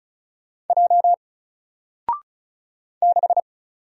Morse code, heard as a keyed tone, about 36 wpm, 700 Hz. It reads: J 6
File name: Single-Letter-Number-35wpm